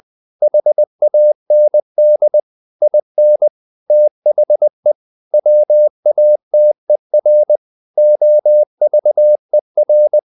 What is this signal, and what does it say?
Morse code, heard as a keyed tone, about 20 wpm, 600 Hz. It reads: HAND IN THE WATER OVER
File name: the old man and the sea part 2 20wpm